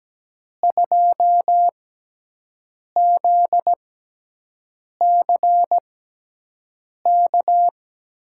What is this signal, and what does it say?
Morse code, heard as a keyed tone, about 17 wpm, spaced as for 6 wpm, 700 Hz. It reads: 2ZCK
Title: USA Call Signs - 6 Characters - 6x Character Spacing - 17wpm